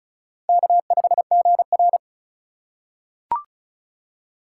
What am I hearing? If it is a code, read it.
Morse code, heard as a keyed tone, about 34 wpm, 700 Hz. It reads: K5GR
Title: USA Call Signs - 6x Character Spacing - 35wpm